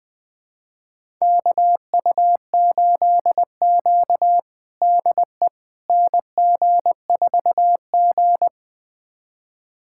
Morse code, heard as a keyed tone, about 20 wpm, 700 Hz. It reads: KU8Q DE NG4G
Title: USA Callsign DE Callsign - 4 Characters - 20wpm